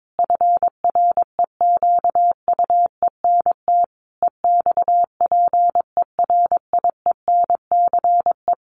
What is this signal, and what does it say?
Morse code, heard as a keyed tone, about 22 wpm, 700 Hz. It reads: FREQUENT EXPERIENCE
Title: Sentences from Top 1400 Words 22wpm